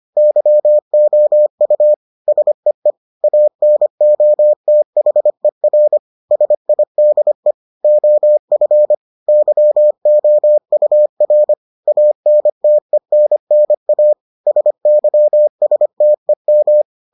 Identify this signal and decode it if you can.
Morse code, heard as a keyed tone, about 25 wpm, 600 Hz. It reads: YOU SEE ANOTHER SIDE OF YOUR ANTENNA SYSTEM